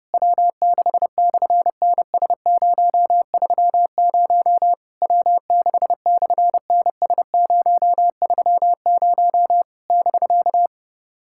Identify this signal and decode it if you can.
Morse code, heard as a keyed tone, about 29 wpm, 700 Hz. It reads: W6/NS030 W6/NS030 <BK>
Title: SOTA QSO Practice 30wpm